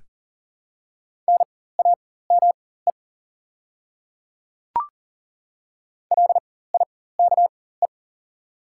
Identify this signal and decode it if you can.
Morse code, heard as a keyed tone, about 40 wpm, spaced as for 19 wpm, 690 Hz. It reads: NAME LIKE
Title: 4x Character Spacing - Top 300 Words - 40wpm